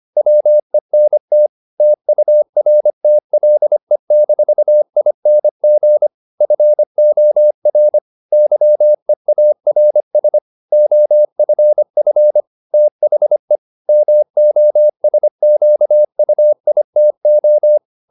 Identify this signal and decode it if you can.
Morse code, heard as a keyed tone, about 25 wpm, 600 Hz. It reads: WENT TURTLE-ING FOR YEARS OFF THE MOSQUITO
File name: the old man and the sea part 1 25wpm